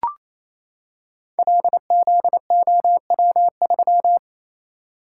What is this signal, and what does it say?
Morse code, heard as a keyed tone, about 27 wpm, 700 Hz. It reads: LZOW3
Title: Code Groups - 5 Letter:Numbers - 28wpm